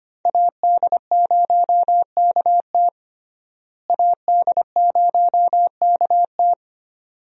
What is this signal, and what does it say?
Morse code, heard as a keyed tone, about 25 wpm, 700 Hz. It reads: AD0KT AD0KT
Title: USA Call Signs - 5 Characters - Repeated 2x - 25wpm